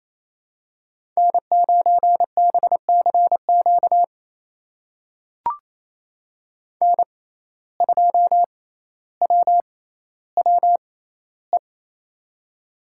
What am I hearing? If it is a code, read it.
Morse code, heard as a keyed tone, about 27 wpm, spaced as for 14 wpm, 700 Hz. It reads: N9BCQ N2WWE
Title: USA Call Signs - 5 Characters - 6x Character Spacing - 28wpm